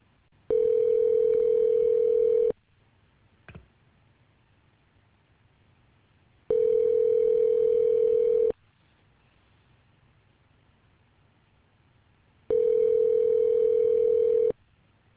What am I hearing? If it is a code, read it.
Ringback tone (North America), 440 + 480 Hz.